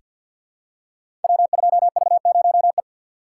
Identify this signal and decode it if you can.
Morse code, heard as a keyed tone, about 49 wpm, 700 Hz. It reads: W2V0E
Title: Code Groups - 5 Letter:Numbers - 50wpm